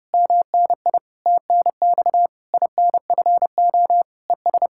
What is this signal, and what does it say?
Morse code, heard as a keyed tone, about 33 wpm, 700 Hz. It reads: MNI TNX INFO ES